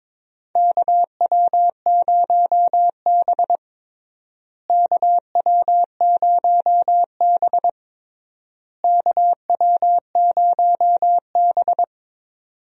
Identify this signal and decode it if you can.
Morse code, heard as a keyed tone, about 22 wpm, 700 Hz. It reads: KW0B KW0B KW0B
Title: USA Call Signs - 4 Characters - Repeated 3x - 22wpm